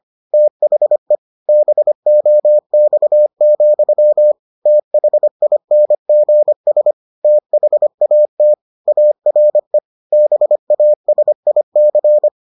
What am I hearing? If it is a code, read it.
Morse code, heard as a keyed tone, about 25 wpm, 600 Hz. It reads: THE BOX, THINGS THAT ARE BASIC